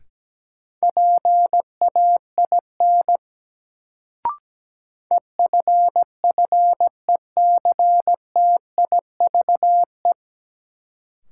Morse code, heard as a keyed tone, about 17 wpm, 700 Hz. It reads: PAIN EFFECTIVE
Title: Sentences from Top 1600 Words - New Word Drill Alt - 17wpm